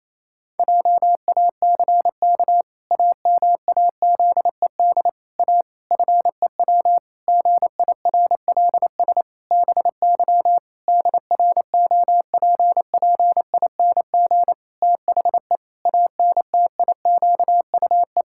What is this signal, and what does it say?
Morse code, heard as a keyed tone, about 28 wpm, 700 Hz. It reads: JACK AMAZED A FEW GIRLS BY DROPPING THE ANTIQUE
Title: Pangram Challenge v2 - 28wpm